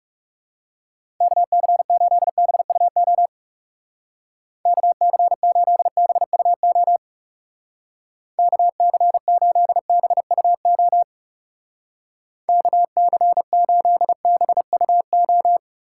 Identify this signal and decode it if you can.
Morse code, heard as a keyed tone, about 37 wpm, 700 Hz. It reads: KC8BUO KC8BUO KC8BUO KC8BUO
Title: Speed Racing - USA Call Signs 30wpm